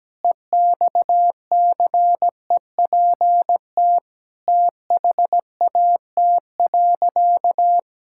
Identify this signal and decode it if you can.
Morse code, heard as a keyed tone, about 17 wpm, 700 Hz. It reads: EXCEPT THAT.
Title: Sentences from Top 1600 Words 17wpm